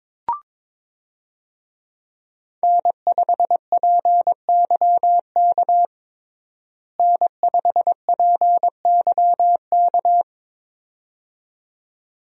Morse code, heard as a keyed tone, about 22 wpm, 700 Hz. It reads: N5PYK N5PYK
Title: USA Call Signs - 5 Characters - Repeated 2x - 22wpm